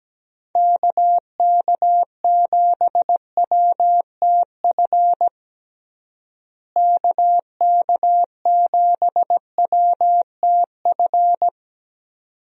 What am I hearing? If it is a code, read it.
Morse code, heard as a keyed tone, about 17 wpm, 700 Hz. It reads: KK7WTF KK7WTF
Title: USA Call Signs - 6 Characters - Repeated 2x - 17wpm